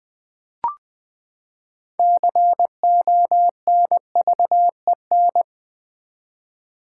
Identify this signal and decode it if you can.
Morse code, heard as a keyed tone, about 20 wpm, 700 Hz. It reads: CONVEN
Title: Build A Word - 6 Letters At A Time - 12-Letter Words - 20wpm